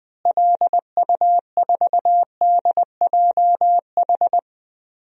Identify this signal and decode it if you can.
Morse code, heard as a keyed tone, about 20 wpm, 700 Hz. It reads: LU4DJH
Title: DXCC-100 Call Signs - 6 Characters - Repeated 2x - 20wpm